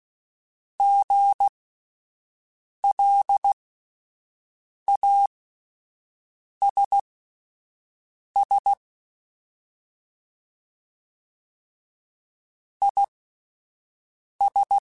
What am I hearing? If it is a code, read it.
Morse code, heard as a keyed tone, about 16 wpm, spaced as for 5 wpm, 800 Hz. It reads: GLASS IS
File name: Spaced / Chuckles Spaced-mp3s / HalfFull sp @10